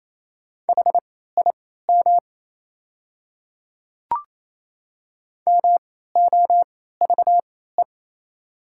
Morse code, heard as a keyed tone, about 28 wpm, spaced as for 16 wpm, 700 Hz. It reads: HIM MOVE
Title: 3x Character Spacing - Top 200 Words - 28wpm